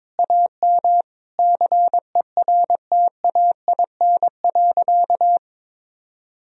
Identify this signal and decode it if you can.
Morse code, heard as a keyed tone, about 22 wpm, 700 Hz. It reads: AM CERTAIN.
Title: Sentences from Top 1000 Words - Review Part 2 - 22wpm